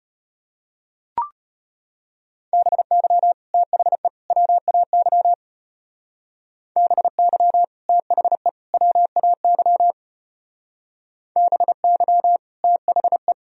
Morse code, heard as a keyed tone, about 31 wpm, 700 Hz. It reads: BY THE WAY BY THE WAY BY THE
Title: Speed-Racing - Sets of 3 Words 25wpm